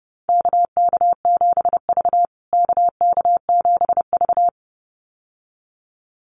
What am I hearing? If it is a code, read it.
Morse code, heard as a keyed tone, about 30 wpm, 700 Hz. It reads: KK7V KK7V
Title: USA Call Signs Repeated 2x 30wpm